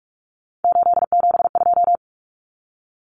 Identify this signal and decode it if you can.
Morse code, heard as a keyed tone, about 44 wpm, 700 Hz. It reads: 872
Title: Common Cut Numbers - 3-Digit 45wpm